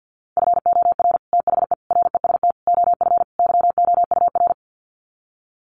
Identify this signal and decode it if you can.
Morse code, heard as a keyed tone, about 49 wpm, 700 Hz. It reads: FOR THE REST OF YOUR
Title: Sets of 5 Words Encore 50wpm